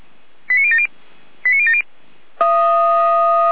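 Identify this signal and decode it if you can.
Five-tone selective call (ZVEI), heard as quick runs of five tones, tones of about 70 ms. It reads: 89070 89070
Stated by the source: ZVEI five-tone call 89058 received by SDR, logged by Feuerwehr Orlamünde.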